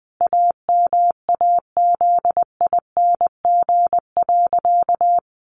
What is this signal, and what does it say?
Morse code, heard as a keyed tone, about 20 wpm, 700 Hz. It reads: AMAZING.
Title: Sentences from Top 1500 Words 20wpm